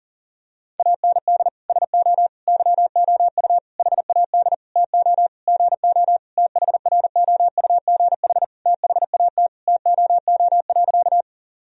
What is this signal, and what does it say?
Morse code, heard as a keyed tone, about 40 wpm, 700 Hz. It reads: AND SO YOU HAD TO GO THROUGH THAT TOO.
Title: Sentences from Top 1000 Words - Review Part 1 - 40wpm